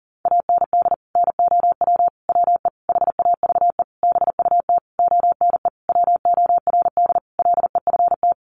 Morse code, heard as a keyed tone, about 40 wpm, 700 Hz. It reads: AND NOW WE HAVE BUT ONE WORD LEFT
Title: Sentences from Top 1500 Words 40wpm